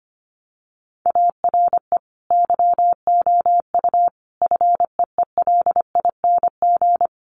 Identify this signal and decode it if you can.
Morse code, heard as a keyed tone, about 25 wpm, 700 Hz. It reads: ARE YOU FEELING
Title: Sentences from Top 1200 Words 25wpm